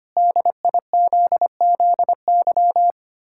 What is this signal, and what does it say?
Morse code, heard as a keyed tone, about 25 wpm, 700 Hz. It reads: DIZZY
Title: Pangram Challenge v2 - 25wpm